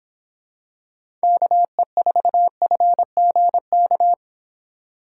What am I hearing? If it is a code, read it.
Morse code, heard as a keyed tone, about 26 wpm, 700 Hz. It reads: KE4FGK